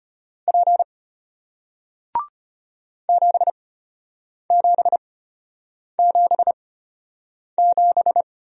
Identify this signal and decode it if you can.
Morse code, heard as a keyed tone, about 33 wpm, spaced as for 8 wpm, 700 Hz. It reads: P 7777